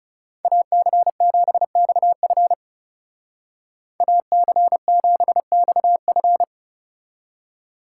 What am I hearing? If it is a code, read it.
Morse code, heard as a keyed tone, about 36 wpm, 700 Hz. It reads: AC7XF AC7XF
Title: Speed Racing - USA Call Signs 30wpm